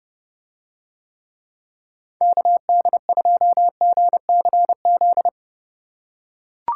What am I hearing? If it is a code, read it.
Morse code, heard as a keyed tone, about 30 wpm, 700 Hz. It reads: KD2GCZ